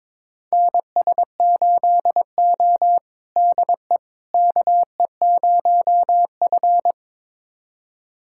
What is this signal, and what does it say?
Morse code, heard as a keyed tone, about 22 wpm, 700 Hz. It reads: NS8O DE KE0F
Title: USA Callsign DE Callsign - 4 Characters - 22wpm